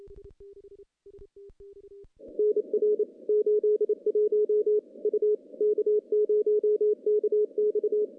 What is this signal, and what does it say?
Morse code, heard as a keyed tone, about 28 wpm, 420 Hz. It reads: NR 81 U K0KX
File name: contests / CW-SS-k5tr- -0000